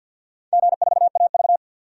Morse code, heard as a keyed tone, about 52 wpm, 700 Hz. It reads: G4AV
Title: DXCC-100 Call Signs - 4 Characters - 50wpm